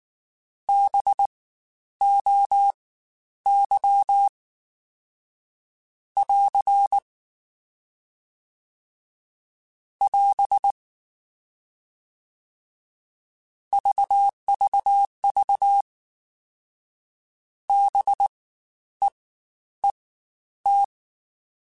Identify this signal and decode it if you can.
Morse code, heard as a keyed tone, about 19 wpm, spaced as for 10 wpm, 800 Hz. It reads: BOY <AR> <AS> VVV BEET
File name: Spaced Alphas / Spaced Alphas mp3s / Bs sp @18